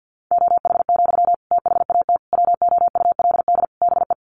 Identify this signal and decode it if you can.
Morse code, heard as a keyed tone, about 48 wpm, 700 Hz. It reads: OH, THAT WOULD BE
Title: Sentences from Top 1100 Words 50wpm